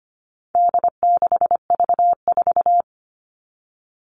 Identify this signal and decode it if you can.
Morse code, heard as a keyed tone, about 25 wpm, 700 Hz. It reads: D6V4
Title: All Cut Numbers - 4-Digit 25wpm